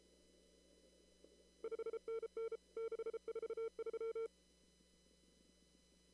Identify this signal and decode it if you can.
Morse code, heard as a keyed tone, about 32 wpm, 450 Hz. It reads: 5NN 643
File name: CW / kt5j wpx cw-op-k5pi